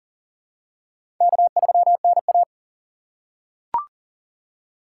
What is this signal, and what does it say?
Morse code, heard as a keyed tone, about 37 wpm, 700 Hz. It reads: K3NA
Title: USA Call Signs - 4 Characters - 2x Character Spacing - 40wpm